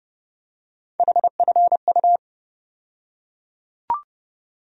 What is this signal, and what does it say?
Morse code, heard as a keyed tone, about 29 wpm, 700 Hz. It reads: HFU
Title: USA Call Sign Postfixes - 3 Letters 30wpm